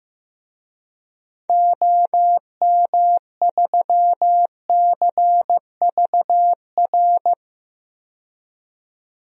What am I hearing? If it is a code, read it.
Morse code, heard as a keyed tone, about 15 wpm, 700 Hz. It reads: OM3CVR
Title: DXCC-100 Call Signs - 6 Characters - 15wpm